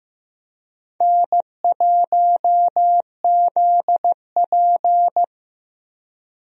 Morse code, heard as a keyed tone, about 15 wpm, 700 Hz. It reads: N1ZP
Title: USA Call Signs - 4 Characters - Repeated 3x - 15wpm